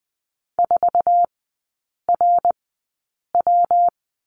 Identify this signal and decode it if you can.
Morse code, heard as a keyed tone, about 20 wpm, spaced as for 8 wpm, 700 Hz. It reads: 4RW
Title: 133 Lesson 22e - Question Mark - Three Characters 20wpm